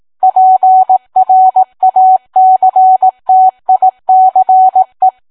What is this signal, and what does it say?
Morse code, heard as a keyed tone, about 18 wpm, 750 Hz. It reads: PRACTICE